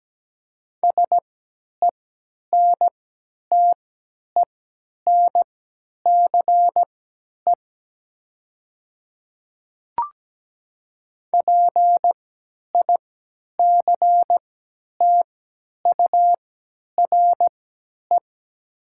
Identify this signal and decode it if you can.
Morse code, heard as a keyed tone, about 17 wpm, spaced as for 10 wpm, 700 Hz. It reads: SENTENCE PICTURE
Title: 3x Character Spacing - Top 200 Words - 17wpm